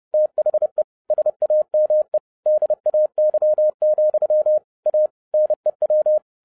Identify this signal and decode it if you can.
Morse code, heard as a keyed tone, about 30 wpm, 610 Hz. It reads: THE SAME DAY, A NEW